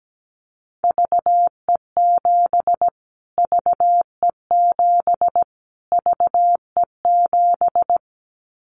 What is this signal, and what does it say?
Morse code, heard as a keyed tone, about 17 wpm, 700 Hz. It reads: VE7 VE7 VE7
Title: Callsign Prefixes for Top 25 DXCC - Repeated 3x - 17wpm